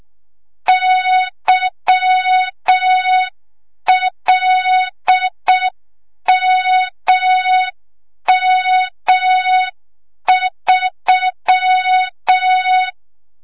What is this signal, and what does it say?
Morse code, heard as a keyed tone, about 6 wpm, 3700 Hz. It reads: YLMM3
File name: Morze-HA2MN / hang / 25 GYAKORLAT